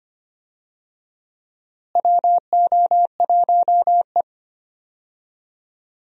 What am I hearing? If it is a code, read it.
Morse code, heard as a keyed tone, about 25 wpm, 700 Hz. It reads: WO1E
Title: USA Call Signs - 4 Characters - Repeated 2x - 25wpm